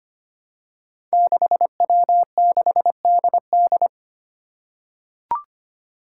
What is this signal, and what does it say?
Morse code, heard as a keyed tone, about 25 wpm, 700 Hz. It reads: 6W6DD
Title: Five Challenging Letter-Number Combinations 25wpm